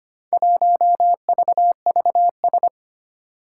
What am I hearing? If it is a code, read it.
Morse code, heard as a keyed tone, about 25 wpm, 700 Hz. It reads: 1VVS
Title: Four Challenging Letter-Number Combinations 25wpm